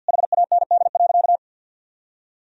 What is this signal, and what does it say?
Morse code, heard as a keyed tone, about 46 wpm, 700 Hz. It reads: HAND.
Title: Sentences from Top 1000 Words - Review Part 2 - 50wpm